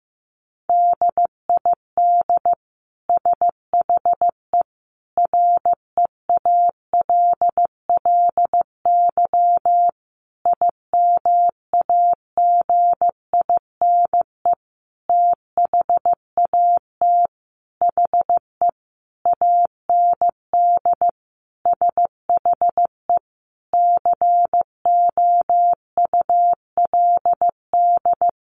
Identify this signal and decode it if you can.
Morse code, heard as a keyed tone, about 15 wpm, 700 Hz. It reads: DID SHE REALLY IMAGINE THAT HE AND SHE COULD
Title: Sentences from Top 1400 Words 15wpm